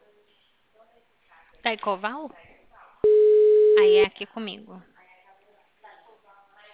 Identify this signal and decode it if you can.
Ringback tone (Europe, 425 Hz), with speech.
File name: out- -2001- - - .10180